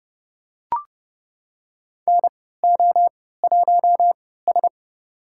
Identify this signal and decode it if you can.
Morse code, heard as a keyed tone, about 30 wpm, spaced as for 17 wpm, 700 Hz. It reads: NO1S
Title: USA Call Signs - 4 Characters - 3x Character Spacing - 30wpm